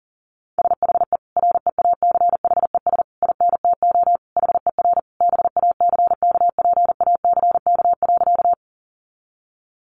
Morse code, heard as a keyed tone, about 40 wpm, 700 Hz. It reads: SHE REACHES INTO HER BACKPACK.
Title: Additional Easy Phrases 40wpm